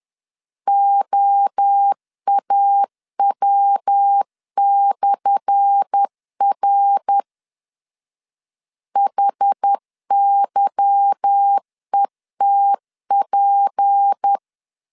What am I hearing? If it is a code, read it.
Morse code, heard as a keyed tone, about 10 wpm, 800 Hz. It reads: OAW/R HYETP
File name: LEZIONI / 10 SIGNS / 10A SLASH 10wpm 2808